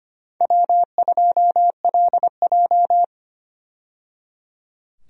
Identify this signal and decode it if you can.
Morse code, heard as a keyed tone, about 25 wpm, 700 Hz. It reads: W2LJ
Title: USA Call Signs - 4 Characters - 25wpm